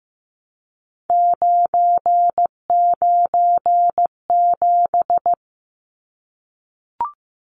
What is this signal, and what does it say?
Morse code, heard as a keyed tone, about 15 wpm, 700 Hz. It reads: 997
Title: Numbers - 3-Digit 15wpm